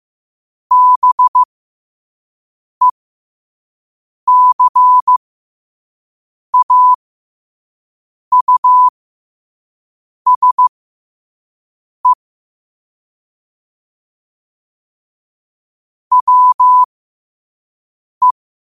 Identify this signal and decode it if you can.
Morse code, heard as a keyed tone, about 15 wpm, spaced as for 5 wpm, 1000 Hz. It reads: BECAUSE WE
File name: qotdfiles / 5wpm